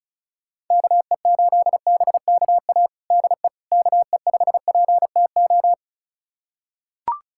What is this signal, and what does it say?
Morse code, heard as a keyed tone, about 36 wpm, 700 Hz. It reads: KE8BKA DE KE5PTO
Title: USA Callsign DE Callsign - 6 Characters - 35wpm